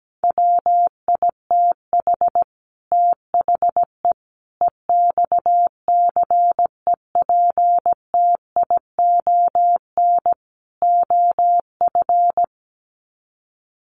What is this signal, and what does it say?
Morse code, heard as a keyed tone, about 17 wpm, 700 Hz. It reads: WITH THE EXCEPTION OF
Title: Sets of 4 Words 17wpm